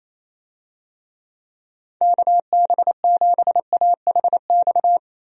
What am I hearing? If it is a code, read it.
Morse code, heard as a keyed tone, about 28 wpm, 700 Hz. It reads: KB7AHX
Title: USA Call Signs - 6 Characters - 28wpm